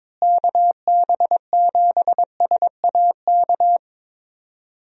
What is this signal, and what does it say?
Morse code, heard as a keyed tone, about 22 wpm, 700 Hz. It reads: KB7SAK
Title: USA Call Signs - 6 Characters - Repeated 2x - 22wpm